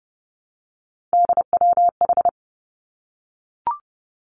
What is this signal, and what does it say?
Morse code, heard as a keyed tone, about 30 wpm, 700 Hz. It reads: DWH
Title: Three Challenging Letter-Number Combinations 30wpm